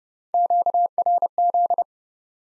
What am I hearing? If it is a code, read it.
Morse code, heard as a keyed tone, about 30 wpm, 700 Hz. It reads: QRZ
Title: Top 90 Abbr 30WPM